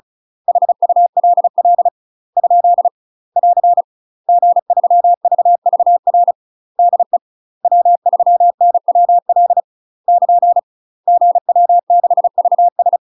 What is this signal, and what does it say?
Morse code, heard as a keyed tone, about 34 wpm, 700 Hz. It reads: HULL ? <AR> G3VVR DE W3NWL <KN> GW6VS
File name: UK MASTER QSOs TRAINING / 35WPM / UK MASTER QSOs 0015